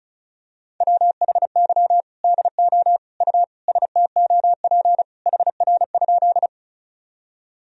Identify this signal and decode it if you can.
Morse code, heard as a keyed tone, about 36 wpm, 700 Hz. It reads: WHY DO U STOP HR?
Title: Sentences from Top 300 Words and Abbreviations 35wpm